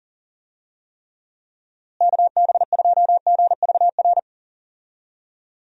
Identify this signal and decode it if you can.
Morse code, heard as a keyed tone, about 40 wpm, 700 Hz. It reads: KB2GVR
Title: USA Call Signs - 6 Characters - Repeated 3x - 40wpm